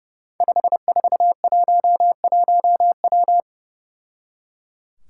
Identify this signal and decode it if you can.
Morse code, heard as a keyed tone, about 30 wpm, 700 Hz. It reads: 5411W
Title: Five Challenging Letter-Number Combinations 30wpm